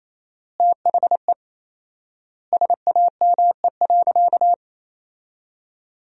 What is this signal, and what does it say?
Morse code, heard as a keyed tone, about 28 wpm, 700 Hz. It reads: THE SAME.
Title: 4x Word-Spacing - Sentences from Top 300 Words - 28wpm